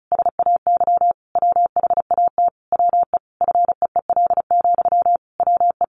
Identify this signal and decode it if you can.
Morse code, heard as a keyed tone, about 34 wpm, 700 Hz. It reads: SAY WHAT WE FEEL, WE
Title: Sentences from Top 1200 Words 35wpm